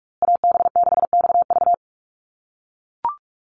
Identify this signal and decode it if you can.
Morse code, heard as a keyed tone, about 44 wpm, 700 Hz. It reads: AB6XV
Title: USA Call Signs Repeated 2x 45wpm